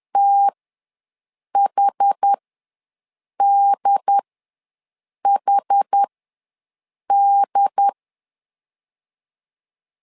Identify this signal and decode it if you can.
Morse code, heard as a keyed tone, about 10 wpm, spaced as for 6 wpm, 800 Hz. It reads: THDHD
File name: LEZIONI / 2 UHDN / 2 UHD ws 10wpm 2058